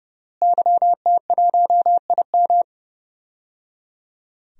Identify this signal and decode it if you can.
Morse code, heard as a keyed tone, about 30 wpm, 700 Hz. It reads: YT1IM
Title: DXCC-100 Call Signs - 5 Characters - Repeated 3x - 30wpm